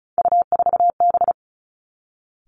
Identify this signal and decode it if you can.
Morse code, heard as a keyed tone, about 36 wpm, 700 Hz. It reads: U4B
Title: Three Challenging Letter-Number Combinations 35wpm